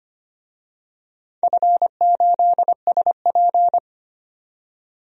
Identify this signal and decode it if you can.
Morse code, heard as a keyed tone, about 25 wpm, 700 Hz. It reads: F8SP
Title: DXCC-100 Call Signs - 4 Characters - 25wpm